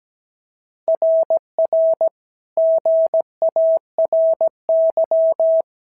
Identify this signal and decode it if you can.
Morse code, heard as a keyed tone, about 16 wpm, 650 Hz. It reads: RR GARY
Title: SOTA QSO Practice 17wpm